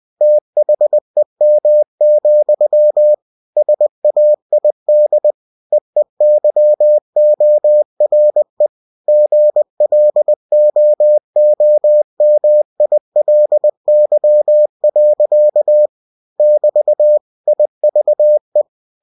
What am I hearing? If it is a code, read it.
Morse code, heard as a keyed tone, about 20 wpm, 600 Hz. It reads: THEM, SAID EEYORE GLOOMILY. <BT> IVE
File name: cw-books / winnie-the-pooh / winnie-20wpm-page-0057